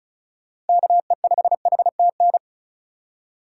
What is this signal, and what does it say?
Morse code, heard as a keyed tone, about 34 wpm, 700 Hz. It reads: KE5HTN
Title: USA Call Signs - 6 Characters - 3x Character Spacing - 35wpm